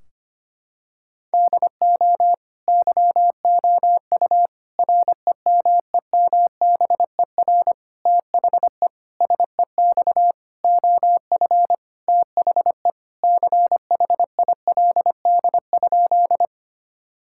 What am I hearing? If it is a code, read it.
Morse code, heard as a keyed tone, about 25 wpm, 700 Hz. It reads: DO YOU REMEMBER THE SEX OF THE CHILD?
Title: Sentences from Top 1600 Words 25wpm